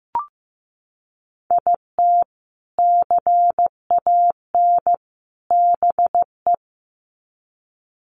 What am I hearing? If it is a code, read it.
Morse code, heard as a keyed tone, about 15 wpm, 700 Hz. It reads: IT CAN BE
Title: Sets of 3 Words 15wpm